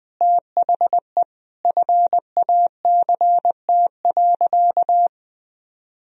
Morse code, heard as a keyed tone, about 21 wpm, 700 Hz. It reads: THE FACT.
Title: Sentences from Top 1000 Words - Review Part 4 - 20wpm